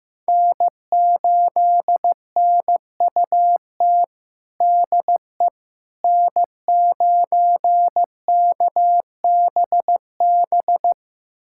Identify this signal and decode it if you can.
Morse code, heard as a keyed tone, about 16 wpm, 700 Hz. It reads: N8NUT DE N9KBB
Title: USA Callsign DE Callsign - 5 Characters - 15wpm